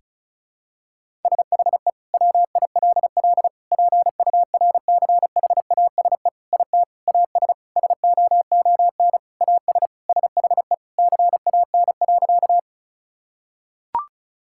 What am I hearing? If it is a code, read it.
Morse code, heard as a keyed tone, about 34 wpm, 700 Hz. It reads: SHE WILL PURCHASE IT AS SOON AS SHE CAN.
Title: Sentences from Top 1000 Words - Review Part 5 - 35wpm